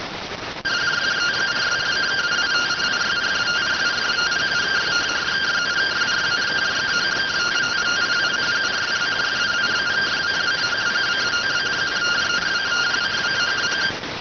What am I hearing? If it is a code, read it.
Two-tone teleprinter signal, about 170 Hz shift.